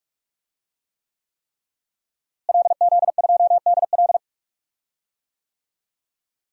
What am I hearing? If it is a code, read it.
Morse code, heard as a keyed tone, about 43 wpm, 700 Hz. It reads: LZ2DL
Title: DXCC-100 Call Signs - 5 Characters - Repeated 3x - 45wpm